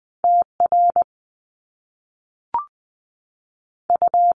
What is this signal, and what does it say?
Morse code, heard as a keyed tone, about 20 wpm, 700 Hz. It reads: T R U
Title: Top Two Letter Combinations in Words 20wpm